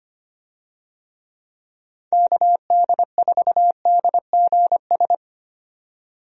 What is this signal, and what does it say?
Morse code, heard as a keyed tone, about 25 wpm, 700 Hz. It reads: KD4DGS
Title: USA Call Signs - 6 Characters - 25wpm